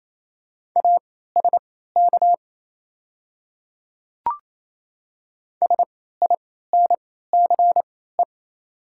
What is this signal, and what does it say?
Morse code, heard as a keyed tone, about 28 wpm, spaced as for 16 wpm, 700 Hz. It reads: ASK SINCE
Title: 3x Character Spacing - Top 500 Words - 28wpm